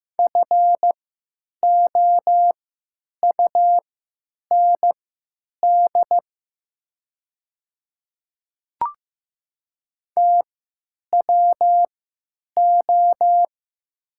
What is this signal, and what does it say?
Morse code, heard as a keyed tone, about 15 wpm, spaced as for 9 wpm, 700 Hz. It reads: FOUND TWO
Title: 3x Character Spacing - Top 300 Words - 15wpm